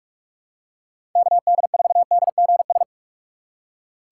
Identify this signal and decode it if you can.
Morse code, heard as a keyed tone, about 47 wpm, 700 Hz. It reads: KD4DGS
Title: USA Call Signs - 6 Characters - 6x Character Spacing - 45wpm